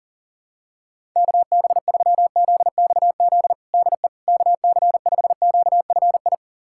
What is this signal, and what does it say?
Morse code, heard as a keyed tone, about 40 wpm, 700 Hz. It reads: KB3ZXZ DE KC5QFI
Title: USA Callsign DE Callsign - 6 Characters - 40wpm